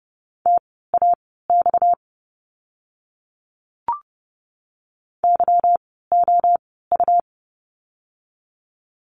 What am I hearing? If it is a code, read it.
Morse code, heard as a keyed tone, about 30 wpm, spaced as for 17 wpm, 700 Hz. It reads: TAX YOU
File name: course / 30:10wpm / 230 Lesson 37f - Letter x - Words 30wpm